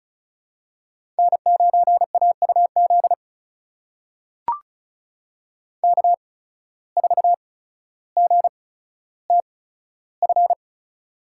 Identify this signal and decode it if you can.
Morse code, heard as a keyed tone, about 36 wpm, spaced as for 15 wpm, 700 Hz. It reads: N9AUZ K4GTF
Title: USA Call Signs - 5 Characters - 8x Character Spacing - 35wpm